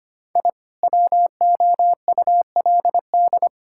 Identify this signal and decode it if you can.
Morse code, heard as a keyed tone, about 25 wpm, 700 Hz. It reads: I WOULD
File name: practice / 25wpm / Sentences with 3-Word Phrases - 25wpm